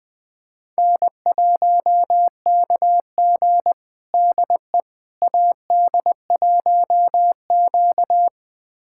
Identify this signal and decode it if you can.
Morse code, heard as a keyed tone, about 20 wpm, 700 Hz. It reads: N1KG DE AD1Q
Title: USA Callsign DE Callsign - 4 Characters - 20wpm